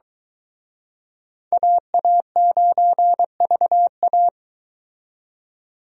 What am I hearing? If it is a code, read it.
Morse code, heard as a keyed tone, about 24 wpm, 700 Hz. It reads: AA9VA